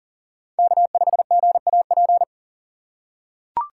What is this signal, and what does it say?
Morse code, heard as a keyed tone, about 40 wpm, 700 Hz. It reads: K5GAP
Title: USA Call Signs - 6x Character Spacing - 40wpm